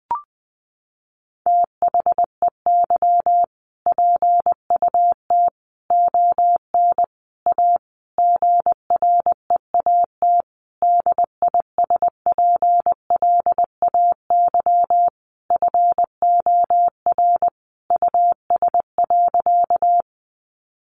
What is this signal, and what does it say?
Morse code, heard as a keyed tone, about 20 wpm, 700 Hz. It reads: THEY PUT ON A GREAT DISPLAY FOR US.
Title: Sentences from Top 1200 Words 20wpm